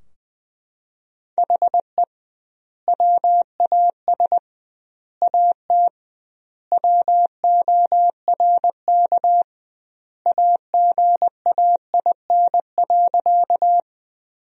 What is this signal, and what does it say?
Morse code, heard as a keyed tone, about 20 wpm, 700 Hz. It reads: HE WAS AT WORK AGAIN.
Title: 2x Word-Spacing - Sentences from Top 200 Words - 20wpm